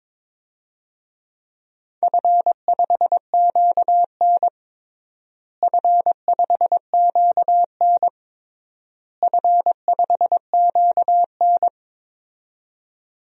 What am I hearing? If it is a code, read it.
Morse code, heard as a keyed tone, about 22 wpm, 700 Hz. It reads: F5QN F5QN F5QN
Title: DXCC-100 Call Signs - 4 Characters - Repeated 3x - 22wpm